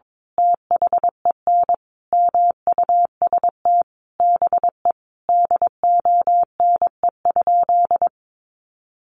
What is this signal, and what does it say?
Morse code, heard as a keyed tone, about 22 wpm, 700 Hz. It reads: THEN MUST BE DONE?
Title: Sentences from Top 500 Words 22wpm